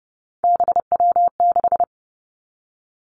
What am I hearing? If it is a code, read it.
Morse code, heard as a keyed tone, about 30 wpm, 700 Hz. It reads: BW6
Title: Three Challenging Letter-Number Combinations 30wpm